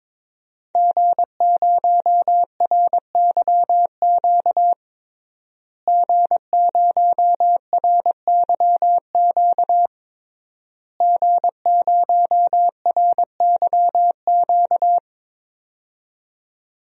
Morse code, heard as a keyed tone, about 22 wpm, 700 Hz. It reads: G0RYQ G0RYQ G0RYQ
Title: DXCC-100 Call Signs - 5 Characters - Repeated 3x - 22wpm